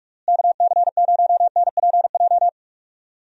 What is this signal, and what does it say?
Morse code, heard as a keyed tone, about 46 wpm, 700 Hz. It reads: KC0NPJ
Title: USA Call Signs - 6 Characters - Repeated 3x - 45wpm